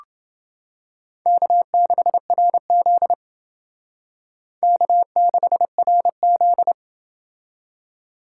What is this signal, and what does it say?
Morse code, heard as a keyed tone, about 28 wpm, 700 Hz. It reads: K6RZ K6RZ